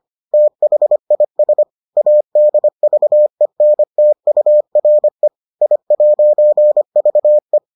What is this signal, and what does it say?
Morse code, heard as a keyed tone, about 25 wpm, 600 Hz. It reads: THIS ADVENTURE I'VE